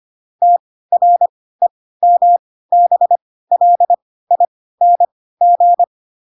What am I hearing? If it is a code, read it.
Morse code, heard as a keyed tone, about 25 wpm, spaced as for 16 wpm, 700 Hz. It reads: TREMBLING
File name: Pinocchio / Pinocchio ch1 10 25 ik1hge